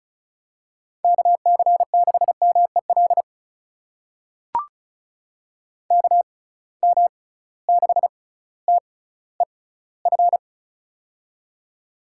Morse code, heard as a keyed tone, about 36 wpm, spaced as for 18 wpm, 700 Hz. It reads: KC6MEL KM6TEF
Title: USA Call Signs - 6 Characters - 6x Character Spacing - 35wpm